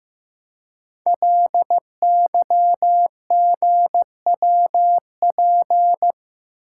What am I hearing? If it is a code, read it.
Morse code, heard as a keyed tone, about 15 wpm, 700 Hz. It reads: LYGWP
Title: Code Groups - 5 Letter:Numbers - 15wpm